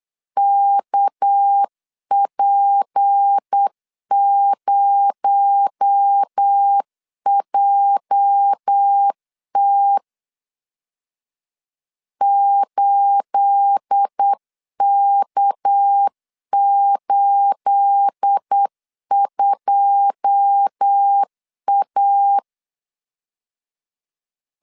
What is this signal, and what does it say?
Morse code, heard as a keyed tone, about 8 wpm, 800 Hz. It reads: KP0JT 8K82A
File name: LEZIONI / 9 28 8wpm 3512